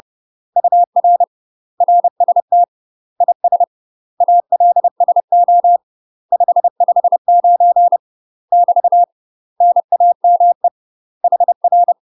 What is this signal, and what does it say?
Morse code, heard as a keyed tone, about 30 wpm, 700 Hz. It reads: UR RST IS ALSO 559 <BT> NAME HR
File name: UK MASTER QSOs TRAINING/30WPM